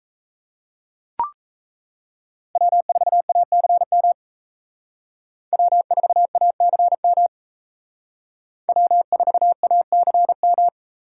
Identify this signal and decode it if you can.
Morse code, heard as a keyed tone, about 34 wpm, 700 Hz. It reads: W4ACM W4ACM W4ACM
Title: Speed Racing - USA 1x3 Call Signs 28wpm